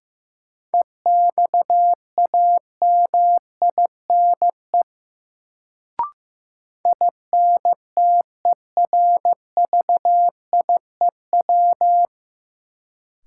Morse code, heard as a keyed tone, about 16 wpm, 700 Hz. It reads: EXAMINE INTERVIEW
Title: Sentences from Top 1600 Words - New Word Drill - 15wpm